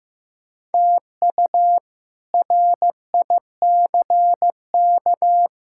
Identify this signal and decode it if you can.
Morse code, heard as a keyed tone, about 15 wpm, 700 Hz. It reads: TU RICK
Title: K1USN SST QSO Practice - 15wpm